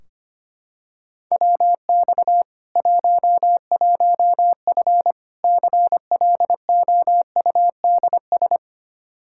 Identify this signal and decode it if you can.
Morse code, heard as a keyed tone, about 25 wpm, 700 Hz. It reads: WX 11F CLOUDS
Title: Practice QSO - Exchange 2 - Info - 25wpm